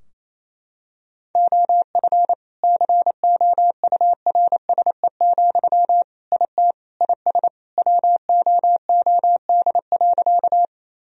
Morse code, heard as a keyed tone, about 28 wpm, 700 Hz. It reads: OF COURSE, IT IS WOOD.
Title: Sentences from Top 1000 Words - Review Part 2 - 28wpm